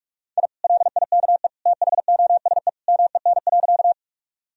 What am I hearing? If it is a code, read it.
Morse code, heard as a keyed tone, about 41 wpm, 700 Hz. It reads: I LIKE THOSE MEN.
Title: Sentences from Top 1000 Words - Review Part 1 - 45wpm